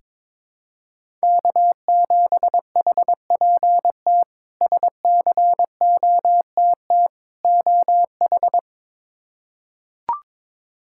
Morse code, heard as a keyed tone, about 22 wpm, 700 Hz. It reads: K7HPT SCOTT OH
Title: K1USN SST QSO Practice - 22wpm